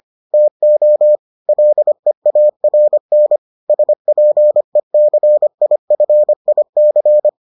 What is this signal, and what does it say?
Morse code, heard as a keyed tone, about 25 wpm, 600 Hz. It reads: TO LEARN SPECIFIC